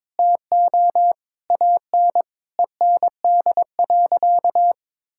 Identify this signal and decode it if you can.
Morse code, heard as a keyed tone, about 22 wpm, 700 Hz. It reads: TO AN END.
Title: Sentences from Top 1600 Words 22wpm